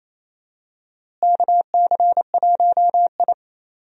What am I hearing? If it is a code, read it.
Morse code, heard as a keyed tone, about 28 wpm, 700 Hz. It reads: KC1I
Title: USA Call Signs - 4 Characters - Repeated 3x - 28wpm